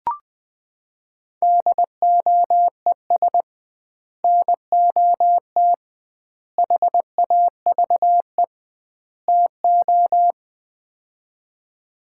Morse code, heard as a keyed tone, about 21 wpm, 700 Hz. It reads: DOES NOT HAVE TO
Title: 2x Word-Spacing - Sets of 4 Words - 20wpm